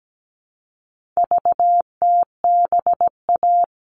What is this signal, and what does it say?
Morse code, heard as a keyed tone, about 17 wpm, 700 Hz. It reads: VTBA
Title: All Cut Numbers - 4-Digit 17wpm